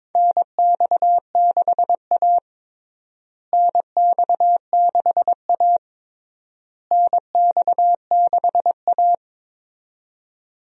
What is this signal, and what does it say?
Morse code, heard as a keyed tone, about 22 wpm, 700 Hz. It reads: NX6A NX6A NX6A
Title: USA Call Signs - 4 Characters - Repeated 3x - 22wpm